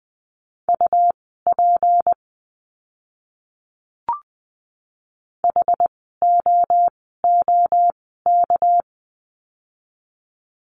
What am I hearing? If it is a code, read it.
Morse code, heard as a keyed tone, about 20 wpm, spaced as for 14 wpm, 700 Hz. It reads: UP HOOK
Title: 199 Lesson 32f - Letter k - Words 20wpm